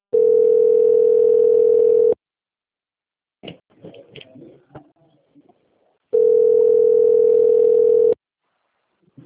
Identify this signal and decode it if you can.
Ringback tone (North America), 440 + 480 Hz.
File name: out- - - - - .7989